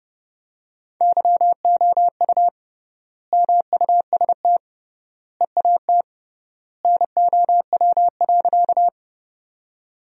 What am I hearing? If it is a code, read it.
Morse code, heard as a keyed tone, about 32 wpm, 700 Hz. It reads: YOU MUST EAT NOW.
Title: 3x Word-Spacing - Sentences from Top 300 Words - 30wpm